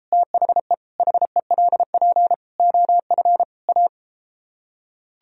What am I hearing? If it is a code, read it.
Morse code, heard as a keyed tone, about 33 wpm, 700 Hz. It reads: THE HELP OF A